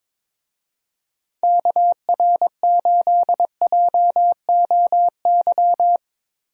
Morse code, heard as a keyed tone, about 22 wpm, 700 Hz. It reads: KR8JOY
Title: USA Call Signs - 6 Characters - Repeated 3x - 22wpm